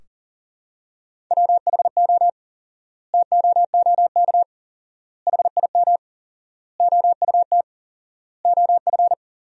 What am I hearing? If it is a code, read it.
Morse code, heard as a keyed tone, about 40 wpm, 700 Hz. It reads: WHO TOOK HIM OUT OF
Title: 4x Word-Spacing - Sentences from Top 300 Words - 40wpm